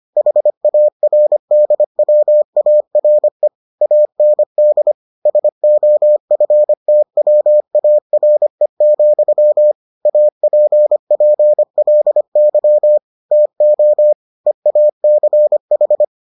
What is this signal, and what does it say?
Morse code, heard as a keyed tone, about 25 wpm, 600 Hz. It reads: HARDWARE AND SOFTWARE, APPLY TO EACH